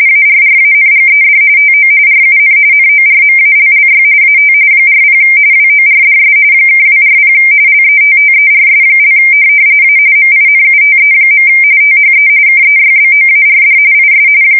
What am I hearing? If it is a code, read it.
RTTY, about 45.45 baud. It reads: ARRANGED TO RUN NORMALLY AT 40 WORDS (240 LETTERS) A MINUTE. THE KEYBOARD IS
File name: fsk170 / morkrum